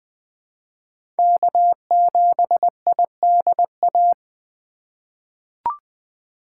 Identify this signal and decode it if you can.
Morse code, heard as a keyed tone, about 21 wpm, 700 Hz. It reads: K7IDA
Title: USA Call Signs - 5 Characters - 2x Character Spacing - 20wpm